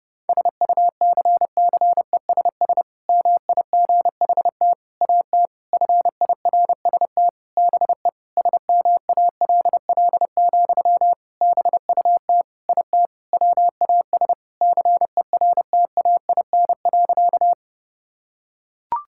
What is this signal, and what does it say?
Morse code, heard as a keyed tone, about 32 wpm, 700 Hz. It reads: SUCCESS MIGHT AT FIRST BE SMALL, BUT IT WAS CERTAIN.
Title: Sentences from Top 1000 Words - Review Part 6 - 30wpm